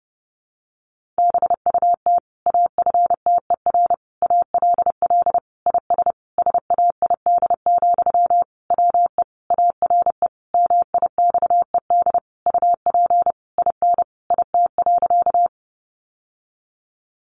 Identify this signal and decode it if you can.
Morse code, heard as a keyed tone, about 30 wpm, 700 Hz. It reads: BUT AFTER ALL IS SAID, WE ARE MIXED UP IN IT.
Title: Sentences from Top 1500 Words 30wpm